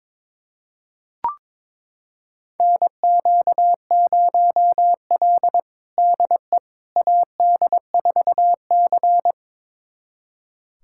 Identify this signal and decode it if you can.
Morse code, heard as a keyed tone, about 22 wpm, 700 Hz. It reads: NQ0L DE AD4C
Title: USA Callsign DE Callsign - 4 Characters - 22wpm